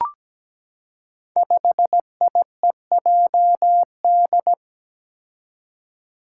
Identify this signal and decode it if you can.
Morse code, heard as a keyed tone, about 17 wpm, 700 Hz. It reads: 5IEJD
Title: Five Challenging Letter-Number Combinations 17wpm